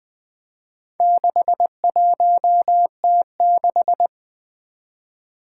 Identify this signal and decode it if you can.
Morse code, heard as a keyed tone, about 20 wpm, 700 Hz. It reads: 61T6
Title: Four Letter-Number Combinations - 20wpm